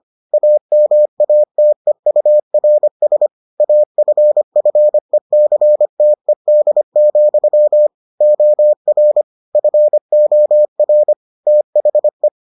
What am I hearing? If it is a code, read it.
Morse code, heard as a keyed tone, about 25 wpm, 600 Hz. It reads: AMATEURS AFFECTED, OR FOR THE